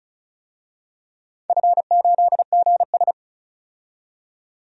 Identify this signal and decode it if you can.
Morse code, heard as a keyed tone, about 34 wpm, 700 Hz. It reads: F8GS
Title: DXCC-100 Call Signs - 4 Characters - 35wpm